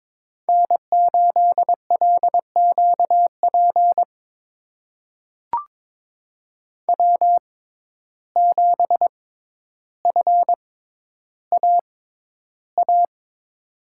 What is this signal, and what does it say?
Morse code, heard as a keyed tone, about 22 wpm, spaced as for 11 wpm, 700 Hz. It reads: N8LQP W7FAA
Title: USA Call Signs - 5 Characters - 6x Character Spacing - 22wpm